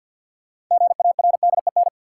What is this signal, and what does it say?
Morse code, heard as a keyed tone, about 49 wpm, 700 Hz. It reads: GARDEN
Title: Sentences from Top 700 Words - New Word Drill - 50wpm